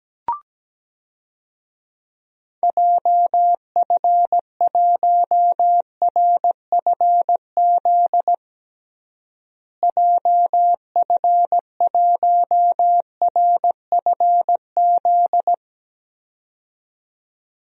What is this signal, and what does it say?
Morse code, heard as a keyed tone, about 17 wpm, 700 Hz. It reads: JF1RFZ JF1RFZ
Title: DXCC-100 Call Signs - 6 Characters - Repeated 2x - 17wpm